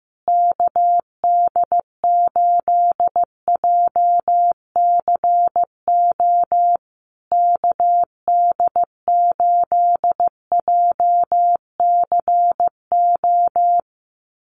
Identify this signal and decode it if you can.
Morse code, heard as a keyed tone, about 15 wpm, 700 Hz. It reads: KD8JCO KD8JCO
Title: USA Call Signs Repeated 3x 15wpm